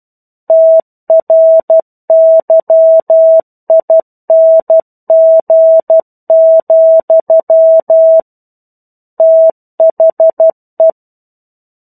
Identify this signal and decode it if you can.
Morse code, heard as a keyed tone, about 12 wpm, 650 Hz. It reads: TRYING, THE
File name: 01-Chapter-01 12 WPM